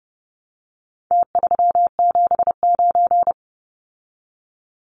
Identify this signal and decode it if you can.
Morse code, heard as a keyed tone, about 30 wpm, 700 Hz. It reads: T379
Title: Numbers - Contest Style - 4-Digit 30wpm